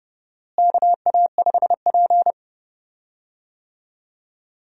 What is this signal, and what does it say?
Morse code, heard as a keyed tone, about 30 wpm, 700 Hz. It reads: KA5P
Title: USA Call Signs - 4 Characters - Repeated 2x - 30wpm